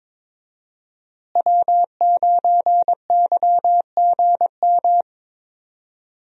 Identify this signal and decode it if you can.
Morse code, heard as a keyed tone, about 22 wpm, 700 Hz. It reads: W9YGM
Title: USA Call Signs - 5 Characters - Repeated 3x - 22wpm